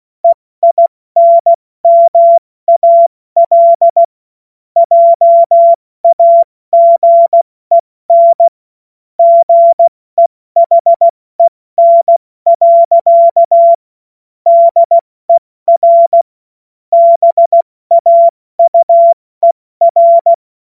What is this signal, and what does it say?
Morse code, heard as a keyed tone, about 16 wpm, spaced as for 14 wpm, 680 Hz. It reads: EINMAL JAGEN GEHEN. DER BAUER
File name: Morsekurs / Witze / Witz 07. März 2023 70 F80